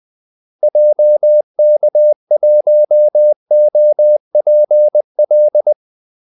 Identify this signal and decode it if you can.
Morse code, heard as a keyed tone, about 20 wpm, 600 Hz. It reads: JK1OPL
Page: Kaldesignaler 20 wpm